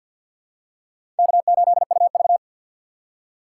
Morse code, heard as a keyed tone, about 48 wpm, 700 Hz. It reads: K8UV
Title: USA Call Signs - 4 Characters - 2x Character Spacing - 50wpm